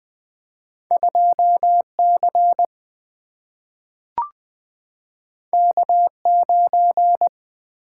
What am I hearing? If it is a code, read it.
Morse code, heard as a keyed tone, about 20 wpm, 700 Hz. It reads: 2C K9
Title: Single-Letter Single-Number 20wpm